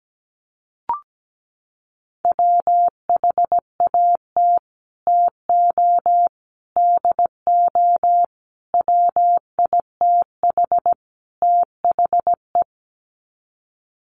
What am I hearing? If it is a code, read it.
Morse code, heard as a keyed tone, about 17 wpm, 700 Hz. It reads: WHAT TO DO WITH THE
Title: Sets of 5 Words Encore2 17wpm